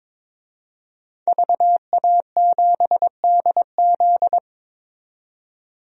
Morse code, heard as a keyed tone, about 22 wpm, 700 Hz. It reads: VA7DZ
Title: DXCC-100 Call Signs - 5 Characters - 22wpm